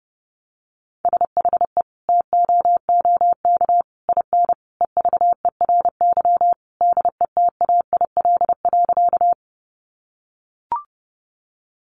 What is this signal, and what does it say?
Morse code, heard as a keyed tone, about 30 wpm, 700 Hz. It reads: SHE TOOK IN EVERY DETAIL.
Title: Sentences from Top 1500 Words 30wpm